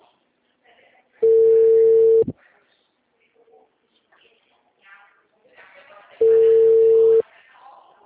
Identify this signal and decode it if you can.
Ringback tone (China, 450 Hz), with speech.
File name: out- -2005- - - .1252